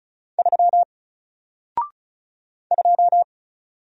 Morse code, heard as a keyed tone, about 36 wpm, 700 Hz. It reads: 3 2
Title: Single Character (Letter:Number:Punctuation:Prosign) 35wpm